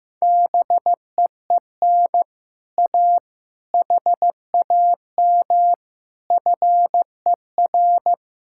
Morse code, heard as a keyed tone, about 15 wpm, 700 Hz. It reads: BEEN A HAM FER
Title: Speed-Racing Common QSO Phrases Encore - 15wpm